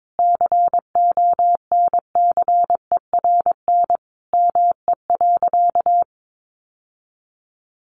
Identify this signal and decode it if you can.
Morse code, heard as a keyed tone, about 22 wpm, 700 Hz. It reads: CONCERN ME.
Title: Sentences from Top 1000 Words 22wpm